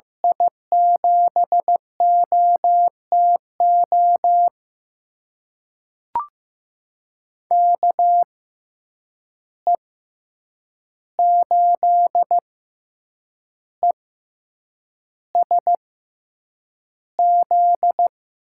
Morse code, heard as a keyed tone, about 15 wpm, spaced as for 7 wpm, 700 Hz. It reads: I7OTO KE8ESZ
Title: USA Call Signs - 6x Character Spacing - 15wpm